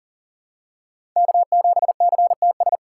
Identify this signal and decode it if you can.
Morse code, heard as a keyed tone, about 40 wpm, 700 Hz. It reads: K7CTS
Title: USA Call Signs - 5 Characters - 2x Character Spacing - 40wpm